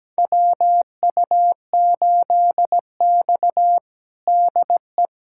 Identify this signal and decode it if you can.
Morse code, heard as a keyed tone, about 17 wpm, 700 Hz. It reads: WU8X DE
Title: USA Callsign DE Callsign - 4 Characters - 17wpm